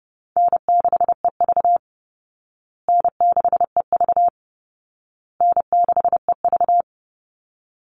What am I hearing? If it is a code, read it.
Morse code, heard as a keyed tone, about 30 wpm, 700 Hz. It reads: N6EV N6EV N6EV
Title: Call Signs Repeated 2x - 30wpm